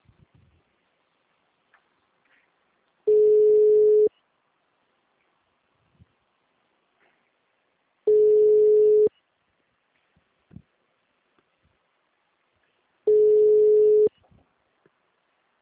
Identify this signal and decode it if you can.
Ringback tone (Europe, 425 Hz), with speech.